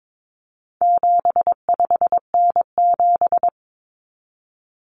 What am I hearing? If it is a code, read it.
Morse code, heard as a keyed tone, about 22 wpm, 700 Hz. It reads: 75N7
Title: Common Cut Numbers - 4-Digit 22wpm